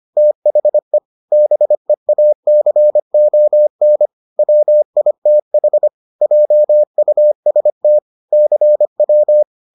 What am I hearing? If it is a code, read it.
Morse code, heard as a keyed tone, about 25 wpm, 600 Hz. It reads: THE BEACON WITH JUST CW